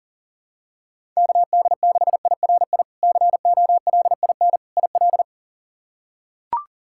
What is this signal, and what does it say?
Morse code, heard as a keyed tone, about 37 wpm, 700 Hz. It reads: KD6IRI COLIN IL
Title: K1USN SST QSO Practice - 40wpm